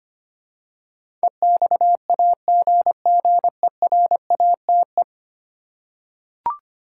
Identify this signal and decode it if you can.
Morse code, heard as a keyed tone, about 25 wpm, 700 Hz. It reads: EXAGGERATE
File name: 10 Letter Words - First 3 Letters Given - 25wpm